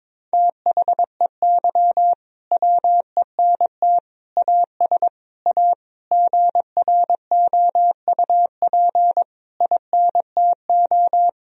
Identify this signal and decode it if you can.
Morse code, heard as a keyed tone, about 22 wpm, 700 Hz. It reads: THEY WENT AS A GROUP INTO
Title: Sentences from Top 1000 Words - Review Part 1 - 22wpm